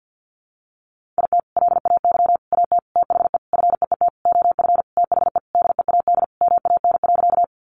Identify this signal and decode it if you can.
Morse code, heard as a keyed tone, about 50 wpm, 700 Hz. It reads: IT LAY AT THE FEET OF THE DEAD MAN.
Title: Sentences from Top 1500 Words 50wpm